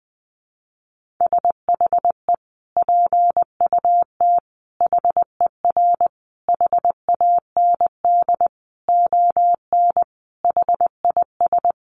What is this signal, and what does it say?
Morse code, heard as a keyed tone, about 20 wpm, 700 Hz. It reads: SHE PUT HER HAND ON HIS
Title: Sentences from Top 1500 Words 20wpm